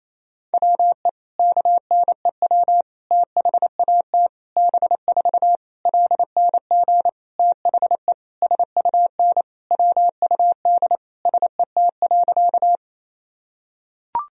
Morse code, heard as a keyed tone, about 27 wpm, 700 Hz. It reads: WE KNEW THAT B4 LNG THE SUN WUD SET.
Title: Sentences from Top 400 Words and Abbreviations 28wpm